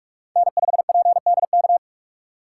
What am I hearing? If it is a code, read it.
Morse code, heard as a keyed tone, about 46 wpm, 700 Hz. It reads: N5PDK
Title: USA Call Signs - 5 Characters - 4x Character Spacing - 45wpm